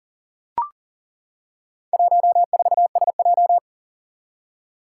Morse code, heard as a keyed tone, about 40 wpm, 700 Hz. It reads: 14SJ
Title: Four Challenging Letter-Number Combinations 40wpm